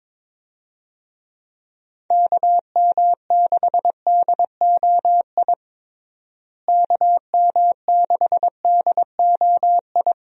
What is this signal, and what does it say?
Morse code, heard as a keyed tone, about 22 wpm, 700 Hz. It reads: KM6DOI KM6DOI
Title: USA Call Signs - 6 Characters - Repeated 3x - 22wpm